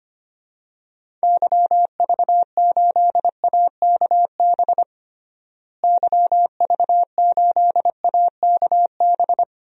Morse code, heard as a keyed tone, about 25 wpm, 700 Hz. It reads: YV8AKB YV8AKB
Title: DXCC-100 Call Signs - 6 Characters - Repeated 3x - 25wpm